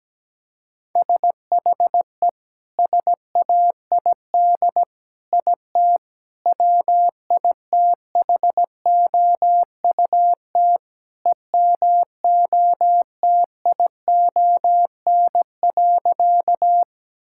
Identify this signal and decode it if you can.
Morse code, heard as a keyed tone, about 17 wpm, 700 Hz. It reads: SHE SAID IT WITHOUT EMOTION.
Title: Sentences from Top 1600 Words 17wpm